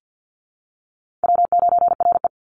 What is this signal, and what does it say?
Morse code, heard as a keyed tone, about 50 wpm, 700 Hz. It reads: W9RE
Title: Easy Call Signs - 50wpm